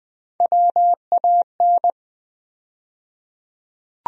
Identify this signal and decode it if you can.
Morse code, heard as a keyed tone, about 20 wpm, 700 Hz. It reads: WAN
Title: USA Call Sign Postfixes - 3 Letters 20wpm